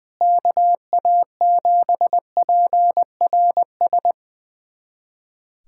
Morse code, heard as a keyed tone, about 21 wpm, 700 Hz. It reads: KA7PRS
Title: USA Callsign DE Callsign - 6 Characters - 20wpm